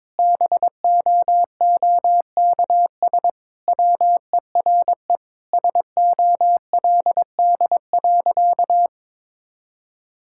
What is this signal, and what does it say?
Morse code, heard as a keyed tone, about 22 wpm, 700 Hz. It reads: BOOKS WERE SOLD.
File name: practice / 22wpm / Sentences with 4-Word Phrases - 22wpm